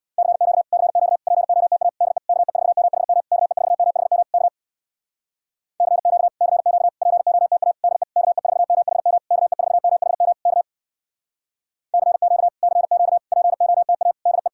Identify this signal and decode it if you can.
Morse code, heard as a keyed tone, about 62 wpm, 700 Hz. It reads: CQ CQ POTA DE KE4MHM KE4MHM K CQ CQ POTA DE KE4MHM KE4MHM K CQ CQ POTA DE
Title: Speed-Racing Common QSO Phrases - 50wpm